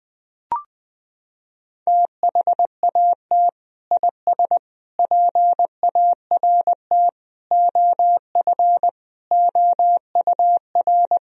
Morse code, heard as a keyed tone, about 20 wpm, 700 Hz. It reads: THAT IS PART OF OUR
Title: Sentences from Top 1000 Words - Review Part 1 - 20wpm